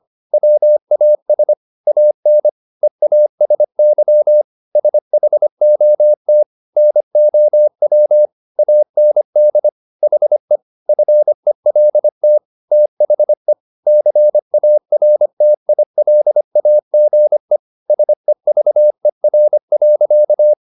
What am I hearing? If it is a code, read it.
Morse code, heard as a keyed tone, about 25 wpm, 600 Hz. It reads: WAS AN EASY SHOT NOW AND HE FELT THE CARTILAGE SEVER.